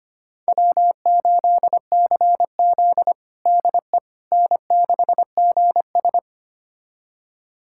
Morse code, heard as a keyed tone, about 25 wpm, 700 Hz. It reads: W8CZ DE N6GS
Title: USA Callsign DE Callsign - 4 Characters - 25wpm